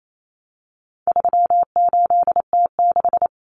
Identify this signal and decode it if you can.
Morse code, heard as a keyed tone, about 28 wpm, 700 Hz. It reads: 38T6
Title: Common Cut Numbers - 4-Digit 28wpm